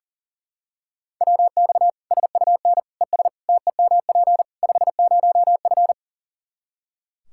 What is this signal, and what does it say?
Morse code, heard as a keyed tone, about 40 wpm, 700 Hz. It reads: WX SUN ES TEMP 50F
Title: Practice QSO - Exchange 2 - Info - 40wpm